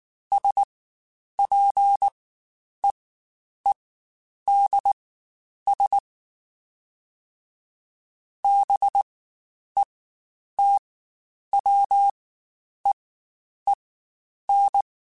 Morse code, heard as a keyed tone, about 19 wpm, spaced as for 9 wpm, 800 Hz. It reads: SPEEDS BETWEEN
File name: Spaced Characters / Spaced Characters-mp3s / Weather 4 14 sp @18